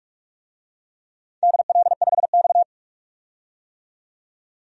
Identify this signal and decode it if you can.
Morse code, heard as a keyed tone, about 44 wpm, 700 Hz. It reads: DL5X
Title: DXCC-100 Call Signs - 4 Characters - 45wpm